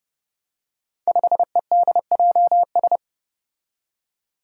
Morse code, heard as a keyed tone, about 30 wpm, 700 Hz. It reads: 5EDJS
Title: Five Challenging Letter-Number Combinations 30wpm